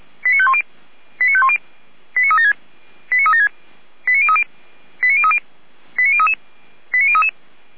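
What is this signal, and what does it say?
Five-tone selective call (ZVEI), heard as quick runs of five tones, tones of about 70 ms. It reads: 89510 89510 89376 89376 89030 89030 89033 89033
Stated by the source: ZVEI five-tone selective call received on SDR by feuerwehr-orlamuende.org, fire service.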